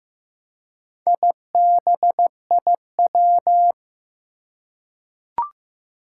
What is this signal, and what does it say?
Morse code, heard as a keyed tone, about 15 wpm, 700 Hz. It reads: IBIW
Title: Four Challenging Letter-Number Combinations 15wpm